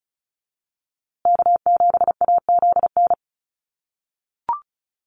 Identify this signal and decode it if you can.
Morse code, heard as a keyed tone, about 34 wpm, 700 Hz. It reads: K7AZN
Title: USA Call Signs Repeated 2x 35wpm